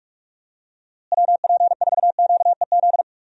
Morse code, heard as a keyed tone, about 42 wpm, 700 Hz. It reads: WP4QEZ
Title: USA Call Signs - 6 Characters - Repeated 2x - 45wpm